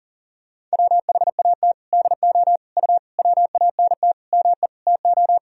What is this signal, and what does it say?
Morse code, heard as a keyed tone, about 40 wpm, 700 Hz. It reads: WHAT DO U WANT ME TO
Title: Sentences from Top 200 Words and Abbreviations 40wpm